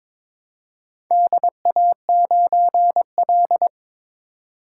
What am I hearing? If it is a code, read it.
Morse code, heard as a keyed tone, about 22 wpm, 690 Hz. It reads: DA9L
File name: practice / 22wpm / Four Letter-Number Combinations - 22wpm